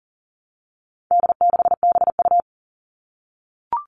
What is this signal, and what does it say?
Morse code, heard as a keyed tone, about 40 wpm, 700 Hz. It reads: D6BU
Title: All Cut Numbers - 4-Digit 40wpm